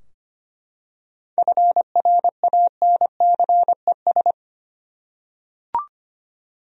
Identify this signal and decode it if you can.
Morse code, heard as a keyed tone, about 25 wpm, 700 Hz. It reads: FRANCES
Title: USA Common Names - 25wpm